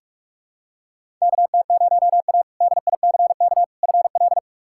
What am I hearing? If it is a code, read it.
Morse code, heard as a keyed tone, about 46 wpm, 700 Hz. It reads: KT0A DICK FL
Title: K1USN SST QSO Practice - 45wpm